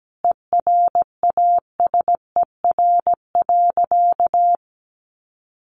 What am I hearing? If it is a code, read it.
Morse code, heard as a keyed tone, about 17 wpm, 700 Hz. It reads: ERASER.
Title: Additional Easy Phrases 17wpm